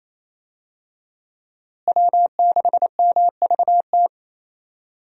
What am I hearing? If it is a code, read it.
Morse code, heard as a keyed tone, about 28 wpm, 700 Hz. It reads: W6MVT
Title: USA Call Signs - 5 Characters - Repeated 2x - 28wpm